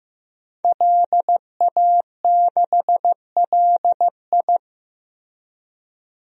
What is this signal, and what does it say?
Morse code, heard as a keyed tone, about 15 wpm, 700 Hz. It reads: LA6LI
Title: DXCC-100 Call Signs - 5 Characters - Repeated 2x - 15wpm